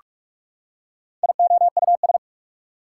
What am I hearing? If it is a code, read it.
Morse code, heard as a keyed tone, about 44 wpm, 690 Hz. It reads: IOUS
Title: Prefixes and Suffixes 45wpm